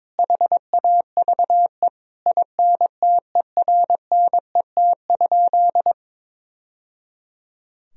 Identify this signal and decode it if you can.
Morse code, heard as a keyed tone, about 21 wpm, 700 Hz. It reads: HAVE INTERNET?
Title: Sentences from Top 1000 Words - Review Part 4 - 22wpm